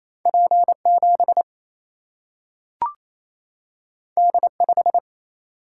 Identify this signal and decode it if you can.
Morse code, heard as a keyed tone, about 28 wpm, 700 Hz. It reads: P7 D5
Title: Single-Letter Single-Number 28wpm